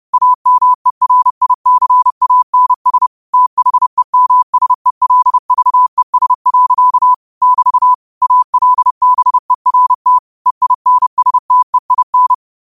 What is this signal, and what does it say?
Morse code, heard as a keyed tone, about 30 wpm, 1000 Hz. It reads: AMERICANS THEMSELVES. <BT> ALBERT EINSTEIN
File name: qotdfiles / 30wpm